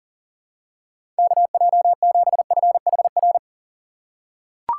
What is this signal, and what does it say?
Morse code, heard as a keyed tone, about 40 wpm, 700 Hz. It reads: KJ7FHR
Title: USA Call Signs - 6x Character Spacing - 40wpm